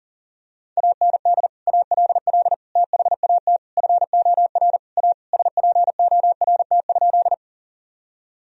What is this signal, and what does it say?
Morse code, heard as a keyed tone, about 37 wpm, 700 Hz. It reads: AND ALL THAT FOR A SPORT?
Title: Sentences from Top 1000 Words - Review Part 4 - 40wpm